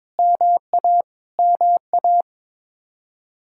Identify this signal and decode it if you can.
Morse code, heard as a keyed tone, about 22 wpm, 700 Hz. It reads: MA MA
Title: Practice QSO - Exchange 1 - Beginning - 22wpm